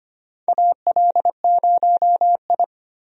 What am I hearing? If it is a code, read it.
Morse code, heard as a keyed tone, about 25 wpm, 700 Hz. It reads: AL0I
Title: USA Call Signs - 4 Characters - 25wpm